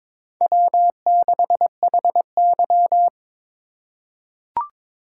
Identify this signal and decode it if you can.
Morse code, heard as a keyed tone, about 23 wpm, 700 Hz. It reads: W6HY
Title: USA Call Signs - 4 Characters - 8x Character Spacing - 22wpm